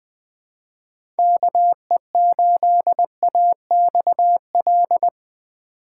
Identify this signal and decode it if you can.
Morse code, heard as a keyed tone, about 20 wpm, 700 Hz. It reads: KE8AXL
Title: USA Call Signs - 6 Characters - 2x Character Spacing - 20wpm